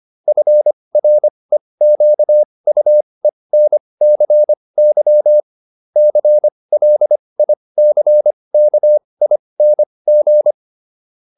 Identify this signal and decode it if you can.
Morse code, heard as a keyed tone, about 25 wpm, spaced as for 20 wpm, 600 Hz. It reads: FREQUENCY CLICKING